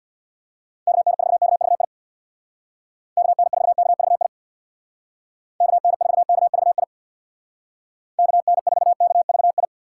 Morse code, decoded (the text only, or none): KN4KVI KN4KVI KN4KVI KN4KVI